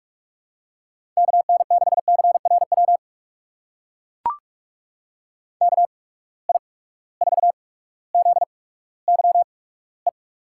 Morse code, heard as a keyed tone, about 43 wpm, spaced as for 19 wpm, 700 Hz. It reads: KN6CRW KI4ZYE
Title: USA Call Signs - 6 Characters - 8x Character Spacing - 45wpm